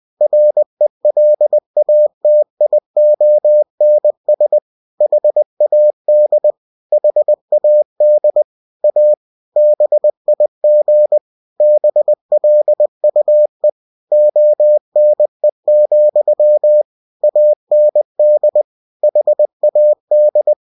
Morse code, heard as a keyed tone, about 20 wpm, 600 Hz. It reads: RELATIONS HAD HAD A BIG BLUE ONE, AND HAD
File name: cw-books / winnie-the-pooh / winnie-20wpm-page-0005